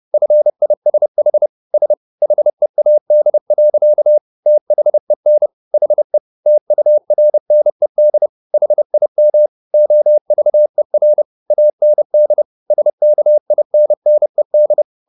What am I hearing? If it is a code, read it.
Morse code, heard as a keyed tone, about 30 wpm, 600 Hz. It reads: FISH S HEAD. THEN HE TURNED HIM OVER AND SKINNED